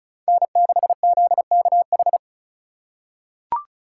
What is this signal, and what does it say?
Morse code, heard as a keyed tone, about 34 wpm, 700 Hz. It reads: N6ZKH
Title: USA Call Signs - 5 Characters - 3x Character Spacing - 35wpm